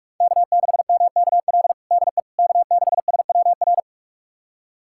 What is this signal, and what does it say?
Morse code, heard as a keyed tone, about 41 wpm, 700 Hz. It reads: K6MKL DE K6SWR
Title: USA Callsign DE Callsign - 5 Characters - 45wpm